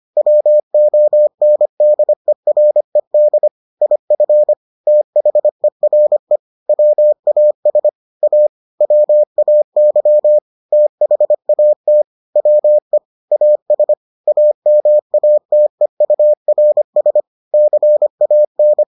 Morse code, heard as a keyed tone, about 25 wpm, 600 Hz. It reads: WONDERED IF THERE WAS A WAY THAT WE AS AMATEURS CAN